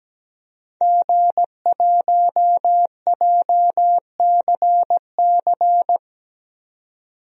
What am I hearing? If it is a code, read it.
Morse code, heard as a keyed tone, about 17 wpm, 700 Hz. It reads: G1JCC
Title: DXCC-100 Call Signs - 5 Characters - 17wpm